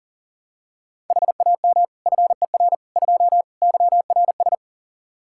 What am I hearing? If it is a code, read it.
Morse code, heard as a keyed tone, about 40 wpm, 700 Hz. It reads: HAM FER 2 YRS
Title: Practice QSO - Exchange 3 - More Info - 40wpm